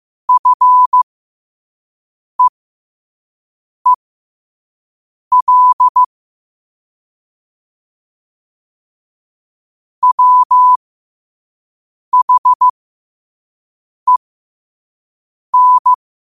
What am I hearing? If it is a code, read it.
Morse code, heard as a keyed tone, about 15 wpm, spaced as for 5 wpm, 1000 Hz. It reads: FEEL WHEN
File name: qotdfiles / 5wpm